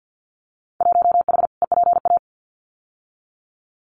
Morse code, heard as a keyed tone, about 50 wpm, 700 Hz. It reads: 1H EPA
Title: Winter Field Day Exchanges 50wpm